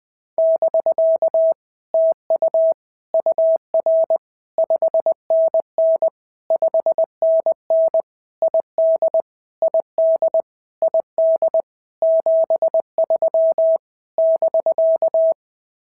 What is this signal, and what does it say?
Morse code, heard as a keyed tone, about 20 wpm, 650 Hz. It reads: <BK> TU UR 5NN 5NN ID ID ID 73 <BK>
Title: POTA QSO Practice - 20wpm